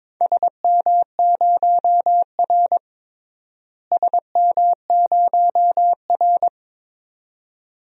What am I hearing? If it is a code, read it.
Morse code, heard as a keyed tone, about 22 wpm, 700 Hz. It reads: SM0R SM0R
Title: DXCC-100 Call Signs - 4 Characters - Repeated 2x - 22wpm